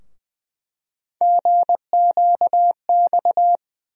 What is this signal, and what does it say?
Morse code, heard as a keyed tone, about 20 wpm, 700 Hz. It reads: GQX
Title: USA Call Sign Postfixes - 3 Letters 20wpm